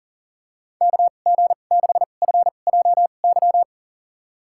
Morse code, heard as a keyed tone, about 40 wpm, spaced as for 29 wpm, 700 Hz. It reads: KG6FJY
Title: USA Call Signs - 6 Characters - 2x Character Spacing - 40wpm